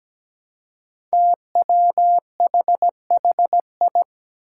Morse code, heard as a keyed tone, about 17 wpm, 700 Hz. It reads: TWHHI
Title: Five Challenging Letter-Number Combinations 17wpm